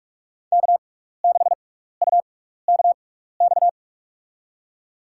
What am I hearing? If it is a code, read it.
Morse code, heard as a keyed tone, about 45 wpm, spaced as for 15 wpm, 690 Hz. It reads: K6UKX
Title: USA Call Signs - 5 Characters - 6x Character Spacing - 45wpm